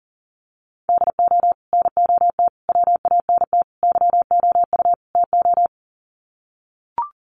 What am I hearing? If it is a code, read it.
Morse code, heard as a keyed tone, about 40 wpm, 700 Hz. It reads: DO NOT WANT YOU TO
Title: Sets of 5 Words 40wpm